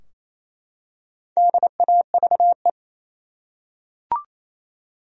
Morse code, heard as a keyed tone, about 28 wpm, 700 Hz. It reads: DAVE
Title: USA Common Names - 28wpm